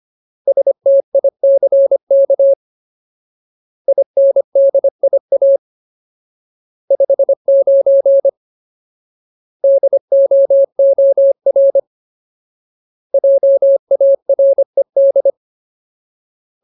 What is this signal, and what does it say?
Morse code, heard as a keyed tone, about 25 wpm, 550 Hz. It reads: STICK INDIA 59 DOOR JARED
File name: Ultimate Word List / 25WPM Ultimate